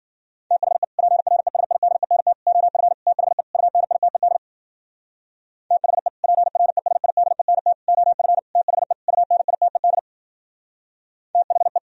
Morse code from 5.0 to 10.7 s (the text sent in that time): THE PRESIDENT OF THE UNITED